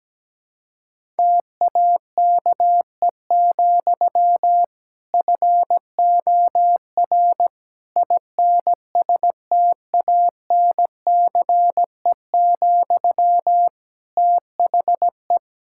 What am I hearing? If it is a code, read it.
Morse code, heard as a keyed tone, about 17 wpm, 700 Hz. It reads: TAKE, FOR INSTANCE, THE
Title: Sentences from Top 1000 Words - Review Part 5 - 17wpm